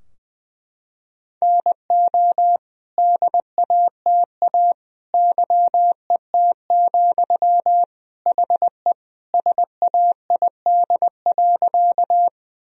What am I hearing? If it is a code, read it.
Morse code, heard as a keyed tone, about 21 wpm, 700 Hz. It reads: NO DATA YET, HE SAID.
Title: Sentences from Top 1000 Words - Review Part 3 - 20wpm